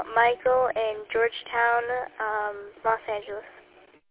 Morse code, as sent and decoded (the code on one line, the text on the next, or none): none